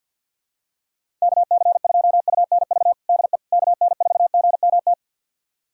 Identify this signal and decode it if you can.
Morse code, heard as a keyed tone, about 53 wpm, 700 Hz. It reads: KC2UNV DE KN4GMT
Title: USA Callsign DE Callsign - 6 Characters - 50wpm